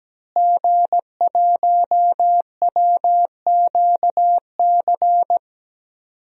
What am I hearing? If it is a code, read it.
Morse code, heard as a keyed tone, about 17 wpm, 700 Hz. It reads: G1WQC
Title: DXCC-100 Call Signs - 5 Characters - Repeated 2x - 17wpm